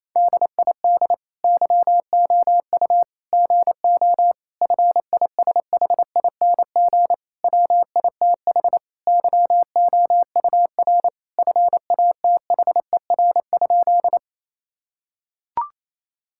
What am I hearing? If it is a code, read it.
Morse code, heard as a keyed tone, about 28 wpm, 700 Hz. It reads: DID YOU GO FISHING WITH YOUR FATHER?
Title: Sentences from Top 1000 Words - Review Part 5 - 28wpm